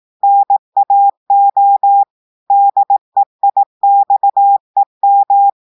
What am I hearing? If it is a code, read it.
Morse code, heard as a keyed tone, about 18 wpm, 800 Hz. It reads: NAO DEIXEM
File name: audios / portugais / portugais texte en clair 2 vitesse 18 mots minutes